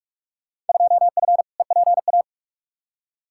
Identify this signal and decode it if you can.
Morse code, heard as a keyed tone, about 43 wpm, 700 Hz. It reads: 2F EPA
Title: Field Day Exchanges - 45wpm